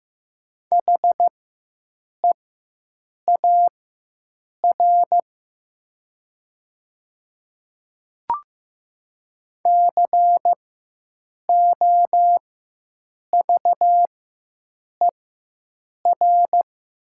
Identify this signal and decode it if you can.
Morse code, heard as a keyed tone, about 15 wpm, spaced as for 7 wpm, 700 Hz. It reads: HEAR COVER
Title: 4x Character Spacing - Top 500 Words - 15wpm